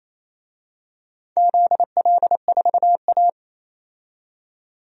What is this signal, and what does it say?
Morse code, heard as a keyed tone, about 28 wpm, 700 Hz. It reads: ZL4A
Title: DXCC-100 Call Signs - 4 Characters - 28wpm